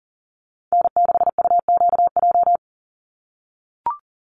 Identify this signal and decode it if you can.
Morse code, heard as a keyed tone, about 40 wpm, 700 Hz. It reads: N6UQJ
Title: USA Call Signs 40wpm